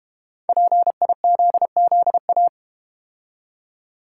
Morse code, heard as a keyed tone, about 32 wpm, 690 Hz. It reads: PIZZA